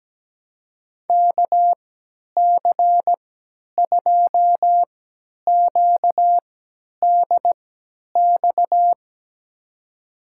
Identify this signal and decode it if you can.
Morse code, heard as a keyed tone, about 17 wpm, spaced as for 10 wpm, 700 Hz. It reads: KC2QDX
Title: USA Call Signs - 6 Characters - 3x Character Spacing - 17wpm